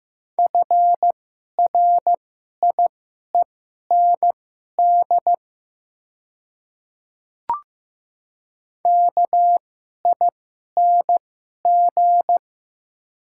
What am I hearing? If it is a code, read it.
Morse code, heard as a keyed tone, about 15 wpm, spaced as for 11 wpm, 700 Hz. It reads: FRIEND KING
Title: 2x Character Spacing - Top 500 Words - 15wpm